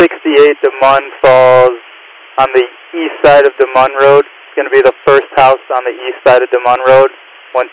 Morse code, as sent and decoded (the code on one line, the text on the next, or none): none